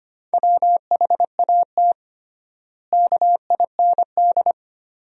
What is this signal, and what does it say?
Morse code, heard as a keyed tone, about 25 wpm, 700 Hz. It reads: WHAT KIND
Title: 3x Word-Spacing - Sentences from Top 500 Words - 25wpm